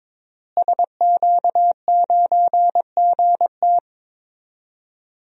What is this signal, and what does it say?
Morse code, heard as a keyed tone, about 22 wpm, 700 Hz. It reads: SQ9GT
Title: DXCC-100 Call Signs - 5 Characters - Repeated 3x - 22wpm